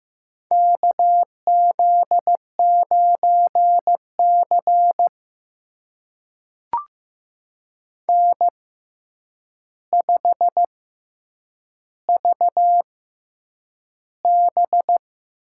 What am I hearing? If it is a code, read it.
Morse code, heard as a keyed tone, about 15 wpm, spaced as for 8 wpm, 700 Hz. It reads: KZ9C N5VB
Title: USA Call Signs - 4 Characters - 6x Character Spacing - 15wpm